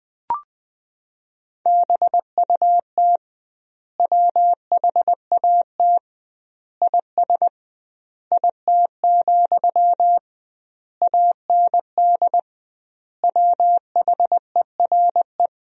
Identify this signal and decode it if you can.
Morse code, heard as a keyed tone, about 21 wpm, 700 Hz. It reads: BUT WHAT IS IT, AND WHERE
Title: 2x Word-Spacing - Sentences from Top 300 Words - 20wpm